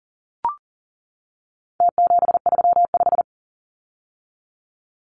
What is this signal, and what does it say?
Morse code, heard as a keyed tone, about 40 wpm, 700 Hz. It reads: T735
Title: Numbers - Contest Style - 4-Digit 40wpm